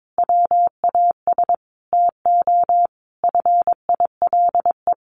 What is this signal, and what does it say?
Morse code, heard as a keyed tone, about 22 wpm, 700 Hz. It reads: WAS TO FILE
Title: Sentences from Top 1300 Words 22wpm